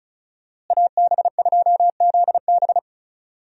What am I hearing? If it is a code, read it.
Morse code, heard as a keyed tone, about 34 wpm, 700 Hz. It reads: AB2ZB
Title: USA Call Signs - 5 Characters - Repeated 3x - 35wpm